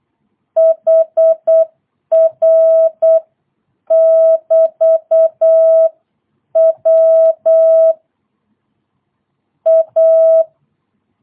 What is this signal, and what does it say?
Morse code, heard as a keyed tone, about 8 wpm, spaced as for 7 wpm, 650 Hz. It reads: HR<BT>W A
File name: jh8cbh / 2022cw